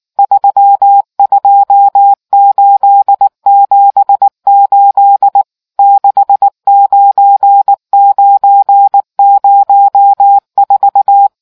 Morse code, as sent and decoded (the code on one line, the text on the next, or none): ...-- ..--- ---.. --... ---.. / -.... ----. ----. ----- ....-
32878 69904